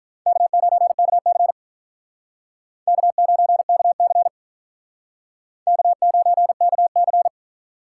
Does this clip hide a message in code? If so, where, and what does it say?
Yes: Morse code from 0.3 to 7.3 s, heard as a keyed tone, about 45 wpm, 700 Hz. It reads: K9KC K9KC K9KC